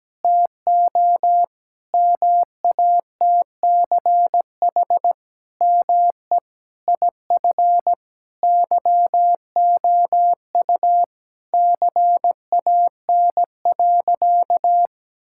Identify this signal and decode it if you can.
Morse code, heard as a keyed tone, about 17 wpm, 700 Hz. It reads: TO MATCH ME IF YOU CAN.
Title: Sentences from Top 1600 Words 17wpm